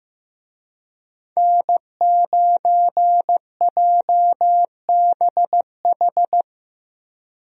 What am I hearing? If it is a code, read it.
Morse code, heard as a keyed tone, about 15 wpm, 700 Hz. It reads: N9JBH
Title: USA Call Signs - 5 Characters - Repeated 2x - 15wpm